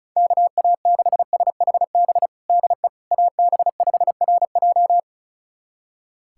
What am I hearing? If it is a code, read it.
Morse code, heard as a keyed tone, about 36 wpm, 700 Hz. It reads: KA6SHB DE AB5RJ
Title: USA Callsign DE Callsign 35wpm